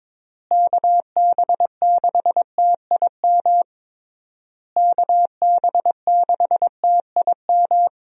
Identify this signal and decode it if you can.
Morse code, heard as a keyed tone, about 22 wpm, 700 Hz. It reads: KB6TIM KB6TIM
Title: USA Call Signs - 6 Characters - Repeated 3x - 22wpm